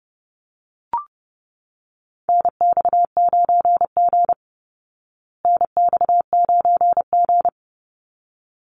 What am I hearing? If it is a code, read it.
Morse code, heard as a keyed tone, about 30 wpm, 700 Hz. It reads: NX9G NX9G
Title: Call Signs Repeated 3x - 30wpm